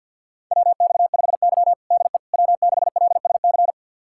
Morse code, heard as a keyed tone, about 45 wpm, 700 Hz. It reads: WX5Y DE W6LSC
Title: USA Callsign DE Callsign 50wpm